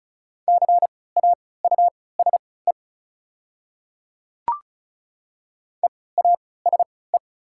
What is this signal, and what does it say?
Morse code, heard as a keyed tone, about 36 wpm, spaced as for 20 wpm, 690 Hz. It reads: CAUSE EASE
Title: 3x Character Spacing - Top 300 Words - 35wpm